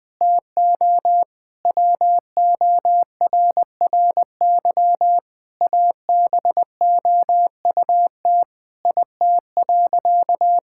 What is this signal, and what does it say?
Morse code, heard as a keyed tone, about 21 wpm, 700 Hz. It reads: TO WORRY ABOUT IT.
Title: Sentences with 7-Word Phrases - Spoken Beginning - 20wpm